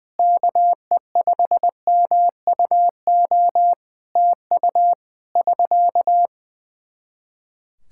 Morse code, heard as a keyed tone, about 20 wpm, 700 Hz. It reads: KE5MUO TU <SK>
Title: Practice QSO - Exchange 4 - Ending - 20wpm